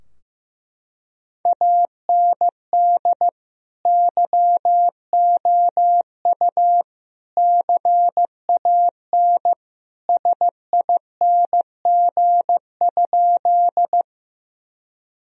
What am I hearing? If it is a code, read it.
Morse code, heard as a keyed tone, about 16 wpm, 700 Hz. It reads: AND YOU CAN SING?
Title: Sentences from Top 1000 Words - Review Part 2 - 15wpm